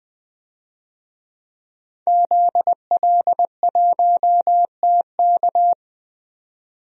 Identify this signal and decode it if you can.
Morse code, heard as a keyed tone, about 20 wpm, 700 Hz. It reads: ZL1TK
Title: DXCC-100 Call Signs - 5 Characters - 20wpm